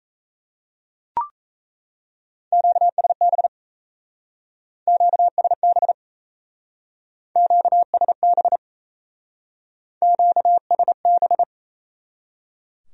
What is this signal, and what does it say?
Morse code, heard as a keyed tone, about 34 wpm, 700 Hz. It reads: QSB QSB QSB QSB
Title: Speed-Racing Q-Codes - 28wpm